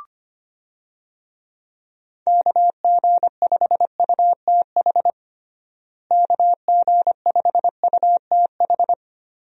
Morse code, heard as a keyed tone, about 25 wpm, 700 Hz. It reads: KG5UTH KG5UTH
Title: USA Call Signs - 6 Characters - Repeated 3x - 25wpm